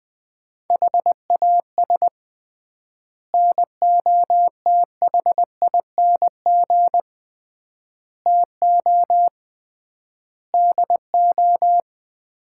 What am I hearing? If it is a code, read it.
Morse code, heard as a keyed tone, about 20 wpm, 700 Hz. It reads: HAS NOTHING TO DO
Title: 3x Word-Spacing - Sets of 5 Words - 20wpm